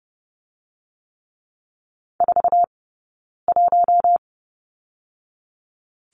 Morse code, heard as a keyed tone, about 30 wpm, 700 Hz. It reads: 4 1
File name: course / 30:10wpm / 099 Lesson 17d - Letter u - Two Characters 30wpm